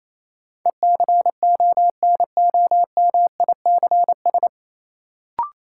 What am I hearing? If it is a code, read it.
Morse code, heard as a keyed tone, about 28 wpm, 700 Hz. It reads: ECONOMICS
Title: Sentences from Top 1000 Words - Review Part 3 - New Word Drill - 28wpm